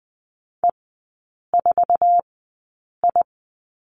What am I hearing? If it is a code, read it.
Morse code, heard as a keyed tone, about 20 wpm, spaced as for 8 wpm, 700 Hz. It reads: E4I
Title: 060 Lesson 11e - Letter h - Three Characters 20wpm